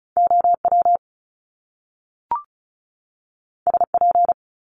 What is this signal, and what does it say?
Morse code, heard as a keyed tone, about 36 wpm, 700 Hz. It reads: OW SP
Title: Top Two Letter Combinations in Words 35wpm